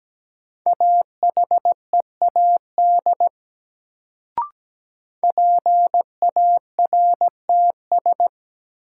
Morse code, heard as a keyed tone, about 17 wpm, 700 Hz. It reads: AHEAD PARTS
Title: Sentences from Top 1000 Words - Review Part 6 - New Word Drill - 17wpm